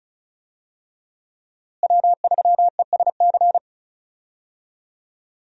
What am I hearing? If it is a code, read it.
Morse code, heard as a keyed tone, about 36 wpm, 700 Hz. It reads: W3ESC
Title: USA Call Signs - 5 Characters - Repeated 2x - 35wpm